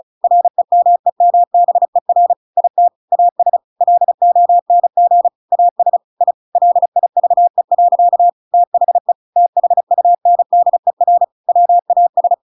Morse code, heard as a keyed tone, about 34 wpm, 700 Hz. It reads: REMEMBER IT AS LONG AS I LIVE. THE THUNDER WAS
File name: Pinocchio / Pinocchio ch7 35 35 ik1hge